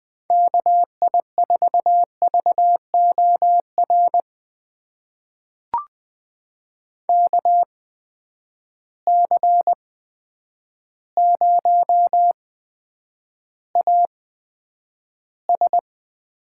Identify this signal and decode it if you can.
Morse code, heard as a keyed tone, about 20 wpm, spaced as for 9 wpm, 700 Hz. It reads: KI4VOR KC0AS
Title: USA Call Signs - 6 Characters - 8x Character Spacing - 20wpm